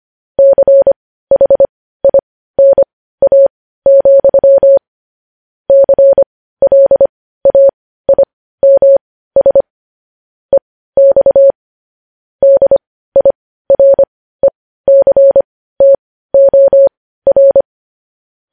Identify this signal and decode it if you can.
Morse code, heard as a keyed tone, about 25 wpm, spaced as for 15 wpm, 550 Hz. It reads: CHINA, CLAIMS EX DIRECTOR
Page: News headlines in Morse code at 15 wpm